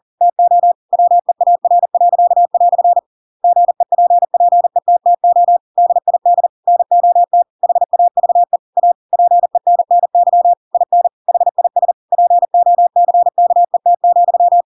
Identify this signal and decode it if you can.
Morse code, heard as a keyed tone, about 40 wpm, 700 Hz. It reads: TO WEAR." GEPPETTO DID NOT HAVE A PENNY IN HIS POCKET,
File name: Pinocchio / Pinocchio ch8 40 40 ik1hge